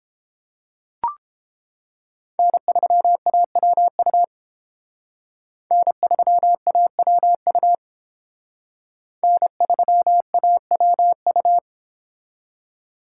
Morse code, heard as a keyed tone, about 28 wpm, 700 Hz. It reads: N3AWU N3AWU N3AWU